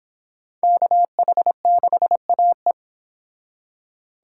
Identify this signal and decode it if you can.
Morse code, heard as a keyed tone, about 25 wpm, 700 Hz. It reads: KH6AE